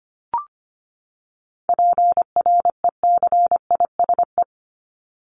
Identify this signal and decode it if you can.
Morse code, heard as a keyed tone, about 25 wpm, 700 Hz. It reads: PRECISE
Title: LY Suffix - Top 100 Words with Base Words 25wpm